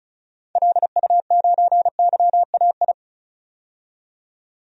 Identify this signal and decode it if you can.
Morse code, heard as a keyed tone, about 36 wpm, 700 Hz. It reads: LU9YAI
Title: DXCC-100 Call Signs - 6 Characters - 35wpm